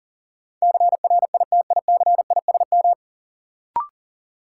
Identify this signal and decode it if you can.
Morse code, heard as a keyed tone, about 40 wpm, 700 Hz. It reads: CRITICISM
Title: Sentences from Top 1300 Words - New Word Drill - 40wpm